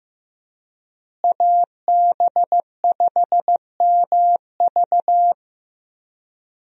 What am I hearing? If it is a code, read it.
Morse code, heard as a keyed tone, about 15 wpm, 700 Hz. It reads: AB5MV
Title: USA Call Signs - 5 Characters - Repeated 3x - 15wpm